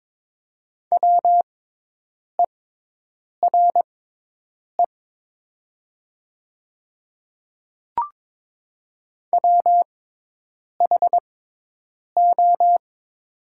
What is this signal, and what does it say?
Morse code, heard as a keyed tone, about 22 wpm, spaced as for 8 wpm, 690 Hz. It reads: WERE WHO
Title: 6x Character Spacing - Top 100 Words - 22wpm